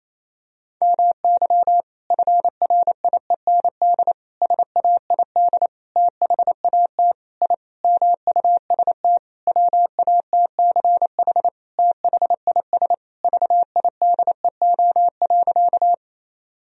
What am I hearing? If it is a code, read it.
Morse code, heard as a keyed tone, about 28 wpm, 700 Hz. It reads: MY FRIEND SAID THAT I MUST WATCH THIS VIDEO.
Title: Sentences from Top 1000 Words - Review Part 4 - 28wpm